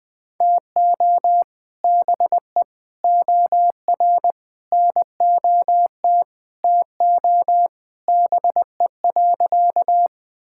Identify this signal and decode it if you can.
Morse code, heard as a keyed tone, about 20 wpm, 700 Hz. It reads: TO BE OR NOT TO BE.
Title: Sentences with 2-Word Phrases - 20wpm